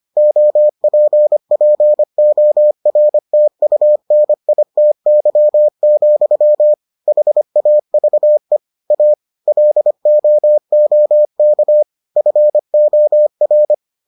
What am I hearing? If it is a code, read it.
Morse code, heard as a keyed tone, about 25 wpm, 600 Hz. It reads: OPPORTUNITY, HAVE A LOOK FOR